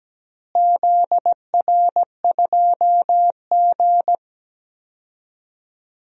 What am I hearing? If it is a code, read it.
Morse code, heard as a keyed tone, about 17 wpm, 700 Hz. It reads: ZR2G
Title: DXCC-100 Call Signs - 4 Characters - Repeated 3x - 17wpm